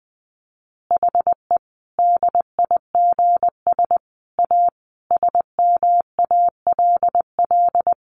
Morse code, heard as a keyed tone, about 20 wpm, 700 Hz. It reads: HE DIGS A SMALL
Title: Easy Phrases 20wpm